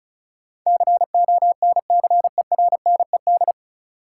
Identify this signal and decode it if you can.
Morse code, heard as a keyed tone, about 36 wpm, 700 Hz. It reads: CONCERNED
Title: Sentences from Top 1000 Words - Review Part 6 - New Word Drill - 35wpm